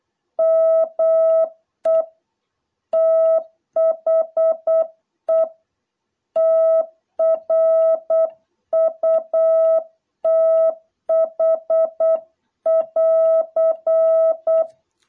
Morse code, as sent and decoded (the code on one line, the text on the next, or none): -- . / - .... / - .-. ..- - .... .-.-.
ME THE TRUTH<AR>